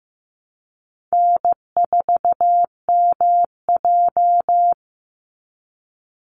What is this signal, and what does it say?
Morse code, heard as a keyed tone, about 15 wpm, 700 Hz. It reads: N4MJ
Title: NA SOTA Chaser Callsigns 15wpm